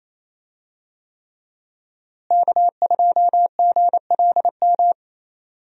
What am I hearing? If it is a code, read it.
Morse code, heard as a keyed tone, about 28 wpm, 700 Hz. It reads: K2GLM
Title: USA Call Signs - 5 Characters - Repeated 3x - 28wpm